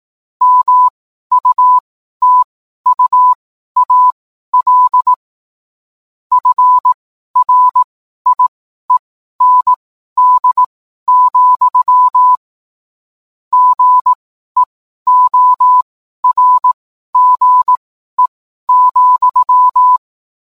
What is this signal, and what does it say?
Morse code, heard as a keyed tone, about 18 wpm, spaced as for 12 wpm, 1000 Hz. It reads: MUTUAL FRIEND, GEORGE,